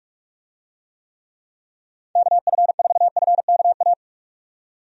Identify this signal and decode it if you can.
Morse code, heard as a keyed tone, about 46 wpm, 700 Hz. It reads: KF4FKA
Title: USA Call Signs - 6 Characters - 45wpm